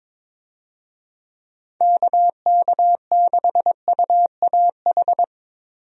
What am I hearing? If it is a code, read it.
Morse code, heard as a keyed tone, about 22 wpm, 700 Hz. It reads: KK6UAH
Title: USA Call Signs - 6 Characters - Repeated 3x - 22wpm